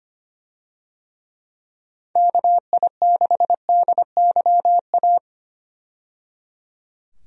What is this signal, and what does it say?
Morse code, heard as a keyed tone, about 25 wpm, 700 Hz. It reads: KI6DYA
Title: USA Call Signs - 6 Characters - 25wpm